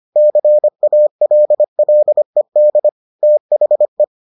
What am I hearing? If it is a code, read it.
Morse code, heard as a keyed tone, about 25 wpm, 600 Hz. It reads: CALLED THE